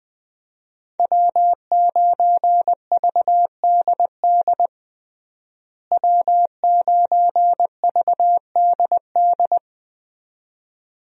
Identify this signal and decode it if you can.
Morse code, heard as a keyed tone, about 20 wpm, 700 Hz. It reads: W9VDD W9VDD
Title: USA Call Signs - 5 Characters - Repeated 2x - 20wpm